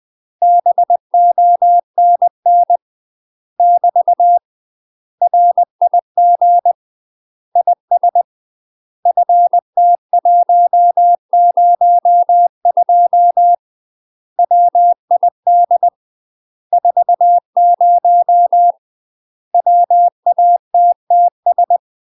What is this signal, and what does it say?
Morse code, heard as a keyed tone, about 20 wpm, 700 Hz. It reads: BONN <BT> RIG IS FT102 WID 40 WATTS
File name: UK MASTER QSOs TRAINING / 20WPM / UK MASTER QSOs 0045